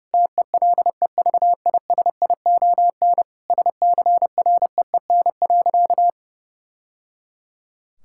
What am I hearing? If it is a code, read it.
Morse code, heard as a keyed tone, about 30 wpm, 700 Hz. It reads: TELEVISION SCREEN.
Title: Sentences from Top 1000 Words - Review Part 6 - 30wpm